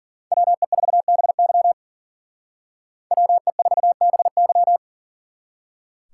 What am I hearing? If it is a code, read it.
Morse code, heard as a keyed tone, about 50 wpm, 700 Hz. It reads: WE4BY WE4BY
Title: Speed Racing - USA 2x2 Call Signs 40wpm